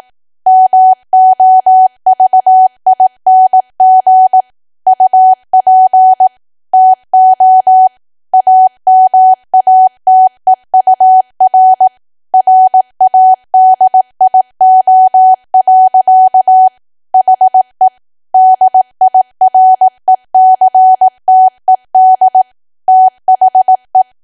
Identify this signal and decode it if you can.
Morse code, heard as a keyed tone, about 18 wpm, 750 Hz. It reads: MOVING UP TO AMATEUR RADIO. HE DIRECTED THE
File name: Archive / 18 WPM / 18WPM